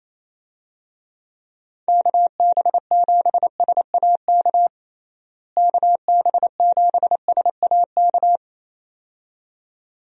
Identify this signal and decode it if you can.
Morse code, heard as a keyed tone, about 28 wpm, 700 Hz. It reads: KB7SAK KB7SAK
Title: USA Call Signs - 6 Characters - Repeated 2x - 28wpm